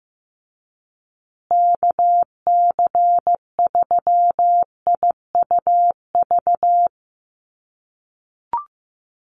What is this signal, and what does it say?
Morse code, heard as a keyed tone, about 15 wpm, 700 Hz. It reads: KC3IUV
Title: USA Call Signs Repeated 2x 15wpm